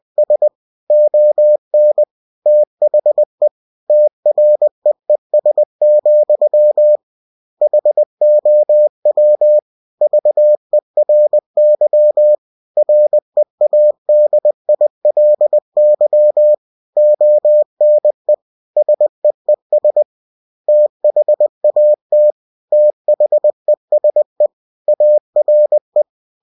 Morse code, heard as a keyed tone, about 20 wpm, 600 Hz. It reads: S ON THE TREES, HOW VERY READILY ONE SEES THAT THESE ARE
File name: cw-books / winnie-the-pooh / winnie-20wpm-page-0068